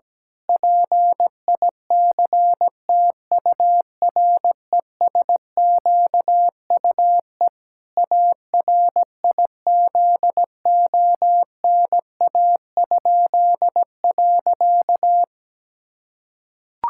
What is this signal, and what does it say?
Morse code, heard as a keyed tone, about 17 wpm, 700 Hz. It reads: PICTURESQUE ARIZONA?.
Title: Pangram Challenge v2 - 17wpm